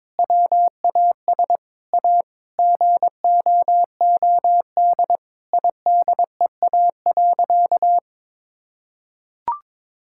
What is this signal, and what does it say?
Morse code, heard as a keyed tone, about 22 wpm, 700 Hz. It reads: WAS A GOOD IDEA.
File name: practice / 22wpm / Sentences with 4-Word Phrases - 22wpm